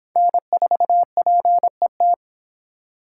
Morse code, heard as a keyed tone, about 23 wpm, 700 Hz. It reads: N4PET